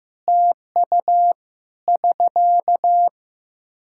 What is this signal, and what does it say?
Morse code, heard as a keyed tone, about 15 wpm, 700 Hz. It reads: TU <SK>
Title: Practice QSO - Exchange 4 - Ending - 15wpm